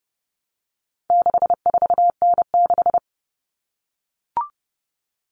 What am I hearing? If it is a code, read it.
Morse code, heard as a keyed tone, about 30 wpm, 700 Hz. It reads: 64N6
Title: All Cut Numbers - 4-Digit 30wpm